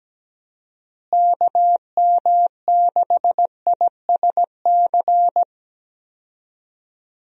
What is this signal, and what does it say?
Morse code, heard as a keyed tone, about 17 wpm, 700 Hz. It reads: KM6ISC
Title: USA Call Signs - 6 Characters - Repeated 3x - 17wpm